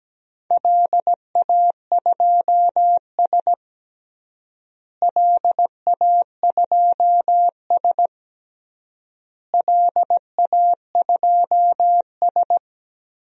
Morse code, heard as a keyed tone, about 17 wpm, 700 Hz. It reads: LA2S LA2S LA2S
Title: DXCC-100 Call Signs - 4 Characters - Repeated 3x - 17wpm